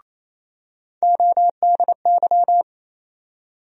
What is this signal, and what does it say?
Morse code, heard as a keyed tone, about 27 wpm, 700 Hz. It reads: ODY
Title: USA Call Sign Postfixes - 3 Letters 28wpm